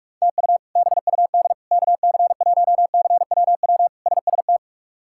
Morse code, heard as a keyed tone, about 46 wpm, 700 Hz. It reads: TU BUD KC1CWW SST
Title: K1USN SST QSO Practice - 45wpm